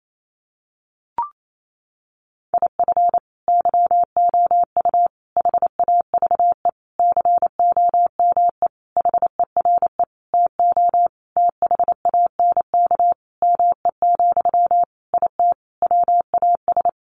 Morse code, heard as a keyed tone, about 28 wpm, 700 Hz. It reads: IF YOU HAVE COME HERE TO THANK ME, IT WAS
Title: Sentences from Top 800 Words 28wpm